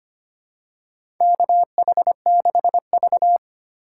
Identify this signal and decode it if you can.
Morse code, heard as a keyed tone, about 25 wpm, 700 Hz. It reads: KH6V
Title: USA Call Signs - 3x Character Spacing - 25wpm